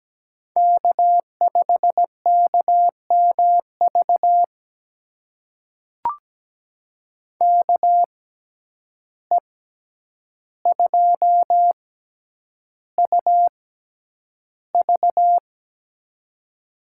Morse code, heard as a keyed tone, about 17 wpm, spaced as for 9 wpm, 700 Hz. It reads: K5KMV KE2UV
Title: USA Call Signs - 5 Characters - 6x Character Spacing - 17wpm